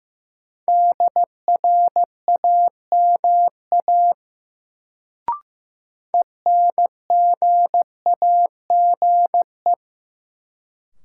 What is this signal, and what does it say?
Morse code, heard as a keyed tone, about 16 wpm, 700 Hz. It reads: DRAMA ENGAGE
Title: Sentences from Top 1500 Words - New Word Drill - 15wpm